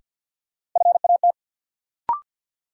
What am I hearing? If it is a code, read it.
Morse code, heard as a keyed tone, about 52 wpm, 700 Hz. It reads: FAT
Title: Sentences from Top 600 Words - New Word Drill - 50wpm